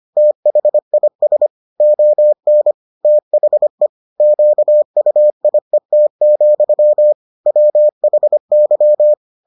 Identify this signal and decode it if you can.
Morse code, heard as a keyed tone, about 25 wpm, 600 Hz. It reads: THIS ON THE QUIET, WHY